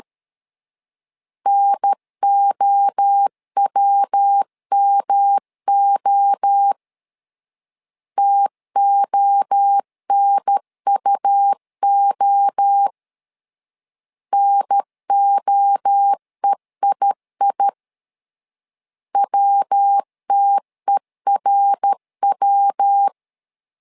Morse code, heard as a keyed tone, about 13 wpm, 800 Hz. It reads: NOWMO TONUO NOEII WTERW
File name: LEZIONI / 3 MWRO / 3 MWRO 12wpm 1730